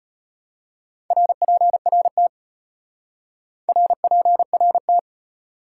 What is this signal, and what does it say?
Morse code, heard as a keyed tone, about 33 wpm, 700 Hz. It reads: RPRT RPRT